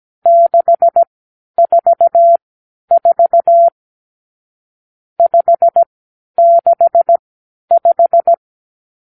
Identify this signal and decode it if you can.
Morse code, heard as a keyed tone, about 17 wpm, spaced as for 10 wpm, 680 Hz. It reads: 644 565
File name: k7qo / Lesson 54 (9)